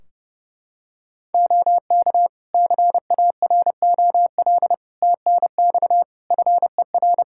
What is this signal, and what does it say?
Morse code, heard as a keyed tone, about 30 wpm, 700 Hz. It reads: OK CAROL TNX FER
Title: K1USN SST QSO Practice 30wpm